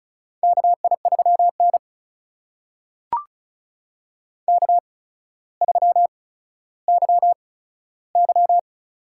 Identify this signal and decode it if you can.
Morse code, heard as a keyed tone, about 36 wpm, spaced as for 15 wpm, 700 Hz. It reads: KI3N K3YY
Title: USA Call Signs - 4 Characters - 8x Character Spacing - 35wpm